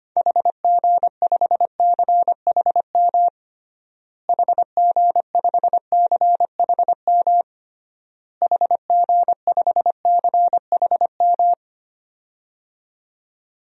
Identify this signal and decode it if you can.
Morse code, heard as a keyed tone, about 25 wpm, 700 Hz. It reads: HG5CHM HG5CHM HG5CHM
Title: DXCC-100 Call Signs - 6 Characters - Repeated 3x - 25wpm